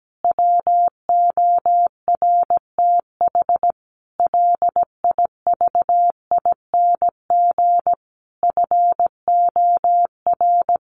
Morse code, heard as a keyed tone, about 17 wpm, 700 Hz. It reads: WORTH LIVING FOR
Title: Sentences from Top 1100 Words 17wpm